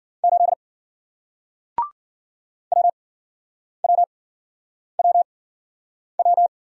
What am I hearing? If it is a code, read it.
Morse code, heard as a keyed tone, about 54 wpm, spaced as for 9 wpm, 700 Hz. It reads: 8 WWWW